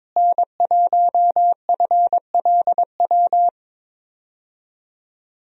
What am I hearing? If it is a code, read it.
Morse code, heard as a keyed tone, about 22 wpm, 700 Hz. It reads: N1FLW
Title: USA Call Signs - 5 Characters - Repeated 3x - 22wpm